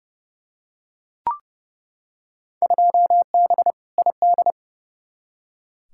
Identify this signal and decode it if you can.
Morse code, heard as a keyed tone, about 30 wpm, 700 Hz. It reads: 2B ID
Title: Field Day Exchanges Encore - 30wpm